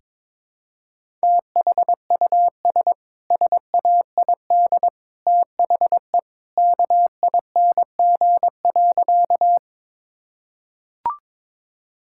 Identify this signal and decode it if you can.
Morse code, heard as a keyed tone, about 22 wpm, 700 Hz. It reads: THUS SAID THE KING.
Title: Sentences from Top 1000 Words - Review Part 5 - 22wpm